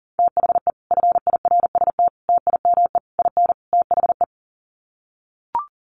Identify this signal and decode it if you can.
Morse code, heard as a keyed tone, about 40 wpm, 700 Hz. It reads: THE FIRST TIME IN THE
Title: Sets of 5 Words Encore 40wpm